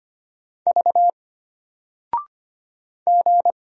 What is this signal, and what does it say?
Morse code, heard as a keyed tone, about 25 wpm, 700 Hz. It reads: V G
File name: Single-Letter-Number-25wpm